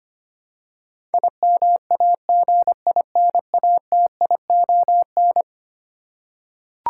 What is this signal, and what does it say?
Morse code, heard as a keyed tone, about 25 wpm, 700 Hz. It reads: IMAGINATION
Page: morse-code-ninja-practice-25wpm